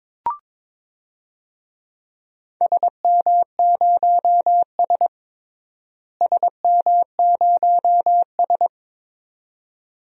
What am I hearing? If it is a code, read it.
Morse code, heard as a keyed tone, about 22 wpm, 700 Hz. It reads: SM0S SM0S
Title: DXCC-100 Call Signs - 4 Characters - Repeated 2x - 22wpm